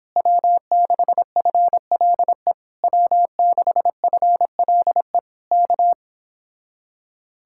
Morse code, heard as a keyed tone, about 26 wpm, 700 Hz. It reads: W6FLE W6FLE K